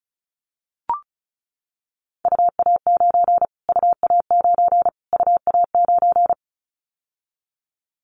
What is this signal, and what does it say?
Morse code, heard as a keyed tone, about 34 wpm, 700 Hz. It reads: UA9 UA9 UA9
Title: Callsign Prefixes for Top 25 DXCC - Repeated 3x - 35wpm